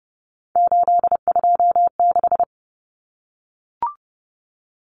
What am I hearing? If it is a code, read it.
Morse code, heard as a keyed tone, about 30 wpm, 700 Hz. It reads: 826
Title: Common Cut Numbers - 3-Digit 30wpm